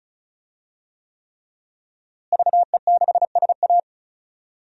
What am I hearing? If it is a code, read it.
Morse code, heard as a keyed tone, about 34 wpm, 700 Hz. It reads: VE6SA
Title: DXCC-100 Call Signs - 5 Characters - Repeated 3x - 35wpm